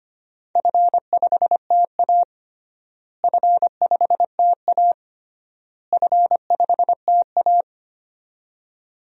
Morse code, heard as a keyed tone, about 25 wpm, 700 Hz. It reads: F5TA F5TA F5TA
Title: DXCC-100 Call Signs - 4 Characters - Repeated 3x - 25wpm